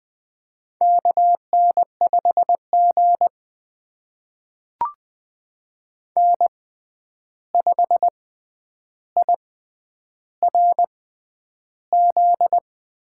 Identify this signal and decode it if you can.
Morse code, heard as a keyed tone, about 20 wpm, spaced as for 10 wpm, 700 Hz. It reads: KN5G N5IRZ
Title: USA Call Signs - 6x Character Spacing - 20wpm